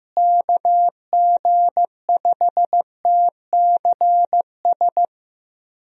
Morse code, heard as a keyed tone, about 15 wpm, 700 Hz. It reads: KG5TCS
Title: USA Call Signs - 6 Characters - Repeated 2x - 15wpm